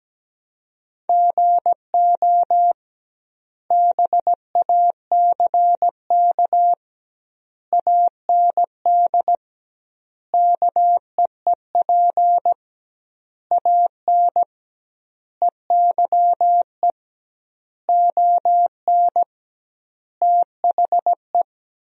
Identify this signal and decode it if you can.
Morse code, heard as a keyed tone, about 17 wpm, 700 Hz. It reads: GO BACK AND KEEP AN EYE ON THE
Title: 2x Word-Spacing - Sentences from Top 300 Words - 17wpm